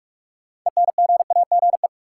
Morse code, heard as a keyed tone, about 46 wpm, 700 Hz. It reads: ENGAGE
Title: Sentences from Top 1500 Words - New Word Drill Alt - 45wpm